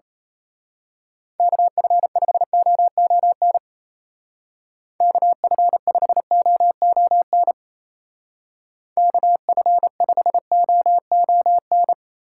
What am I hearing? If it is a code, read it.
Morse code, heard as a keyed tone, about 32 wpm, 700 Hz. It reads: KF5OON KF5OON KF5OON
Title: Speed Racing - USA 2x3 Call Signs 28wpm